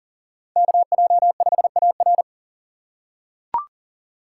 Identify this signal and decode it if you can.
Morse code, heard as a keyed tone, about 40 wpm, 700 Hz. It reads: KJ5AR
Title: USA Call Signs - 5 Characters - 4x Character Spacing - 40wpm